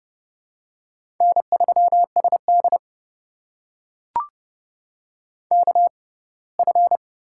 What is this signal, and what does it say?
Morse code, heard as a keyed tone, about 29 wpm, spaced as for 20 wpm, 700 Hz. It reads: N3SD KF
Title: USA Call Signs - 4 Characters - 6x Character Spacing - 30wpm